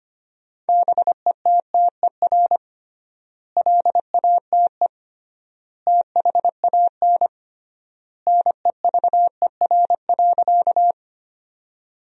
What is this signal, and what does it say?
Morse code, heard as a keyed tone, about 25 wpm, 700 Hz. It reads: BETTER LATE THAN NEVER.
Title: 3x Word-Spacing - Sentences from Top 400 Words - 25wpm